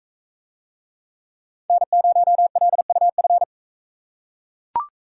Morse code, heard as a keyed tone, about 44 wpm, 700 Hz. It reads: N0LUF